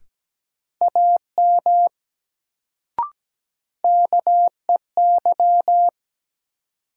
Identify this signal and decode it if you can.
Morse code, heard as a keyed tone, about 17 wpm, 700 Hz. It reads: AM KEY
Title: Sentences from Top 600 Words - New Word Drill - 17wpm